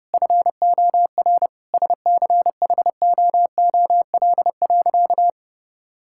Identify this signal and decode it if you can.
Morse code, heard as a keyed tone, about 32 wpm, 700 Hz. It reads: FOR SCHOOL.
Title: Sentences from Top 1000 Words - Review Part 6 - 30wpm